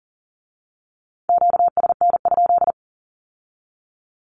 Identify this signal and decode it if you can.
Morse code, heard as a keyed tone, about 38 wpm, 700 Hz. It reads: QSN?
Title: Q-Codes with Spoken Meaning - 40wpm